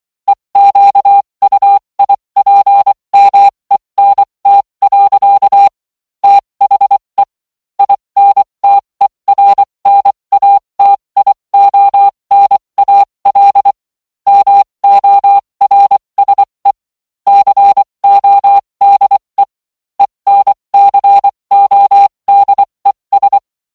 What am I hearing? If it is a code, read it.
Morse code, heard as a keyed tone, about 24 wpm, spaced as for 20 wpm, 800 Hz. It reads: EQUIPMENT. THE INTERNATIONAL MORSE CODE ENCODES